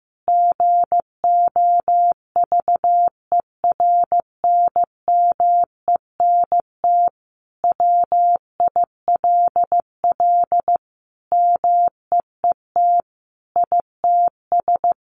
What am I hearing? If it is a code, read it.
Morse code, heard as a keyed tone, about 15 wpm, 700 Hz. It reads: GOVERNMENT WILL MEET ITS
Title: Sentences from Top 1200 Words 15wpm